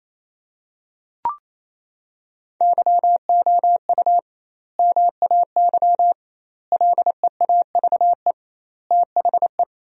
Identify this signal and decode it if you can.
Morse code, heard as a keyed tone, about 28 wpm, 700 Hz. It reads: YOU MAY LEAVE THE
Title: 2x Word-Spacing - Sentences from Top 400 Words - 28wpm